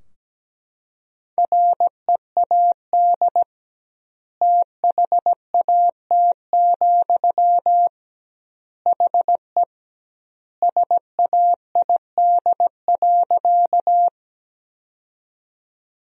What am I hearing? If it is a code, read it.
Morse code, heard as a keyed tone, about 17 wpm, 700 Hz. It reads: READ THAT, HE SAID.
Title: 2x Word-Spacing - Sentences from Top 200 Words - 17wpm